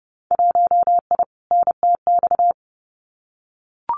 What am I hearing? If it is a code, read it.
Morse code, heard as a keyed tone, about 30 wpm, 700 Hz. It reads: 1I NTX
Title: Winter Field Day Exchanges Encore 30wpm